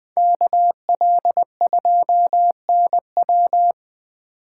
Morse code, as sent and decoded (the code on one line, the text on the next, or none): -.- .-.. ..--- -. .--
KL2NW